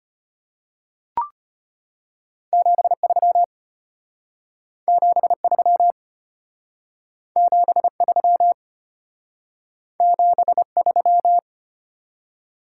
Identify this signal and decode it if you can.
Morse code, heard as a keyed tone, about 33 wpm, 700 Hz. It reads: 73 73 73 73